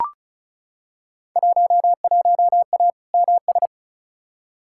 Morse code, heard as a keyed tone, about 34 wpm, 700 Hz. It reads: 11A MS
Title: Field Day Exchanges Encore - 35wpm